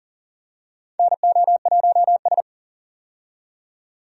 Morse code, heard as a keyed tone, about 40 wpm, 700 Hz. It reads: NO1S
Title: USA Call Signs - 4 Characters - 40wpm